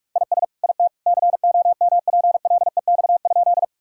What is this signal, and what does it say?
Morse code, heard as a keyed tone, about 46 wpm, 700 Hz. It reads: IS IT COMPLEX?
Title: Sentences from Top 1000 Words - Review Part 6 - 45wpm